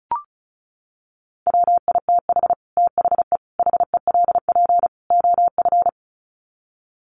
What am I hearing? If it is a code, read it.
Morse code, heard as a keyed tone, about 34 wpm, 700 Hz. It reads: WITH THE HELP OF
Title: Sets of 4 Words 35wpm